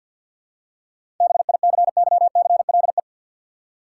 Morse code, heard as a keyed tone, about 47 wpm, 700 Hz. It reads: BICYCLE
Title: Seven Letter Words - 50wpm